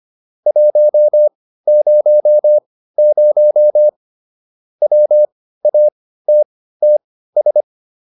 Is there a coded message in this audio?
Yes: Morse code from 0.5 to 7.6 s, heard as a keyed tone, about 25 wpm, spaced as for 15 wpm, 600 Hz. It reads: 100 WATTS